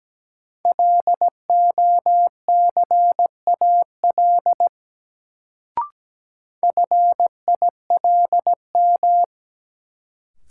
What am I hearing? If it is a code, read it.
Morse code, heard as a keyed tone, about 17 wpm, 700 Hz. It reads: LOCAL FILM
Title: Sentences from Top 1000 Words - Review Part 3 - New Word Drill - 17wpm